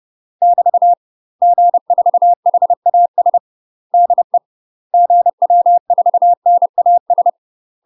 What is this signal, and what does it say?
Morse code, heard as a keyed tone, about 30 wpm, 700 Hz. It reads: <BT> G4HAS DE GW4NAS
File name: UK MASTER QSOs TRAINING / 30WPM / UK MASTER QSOs 0020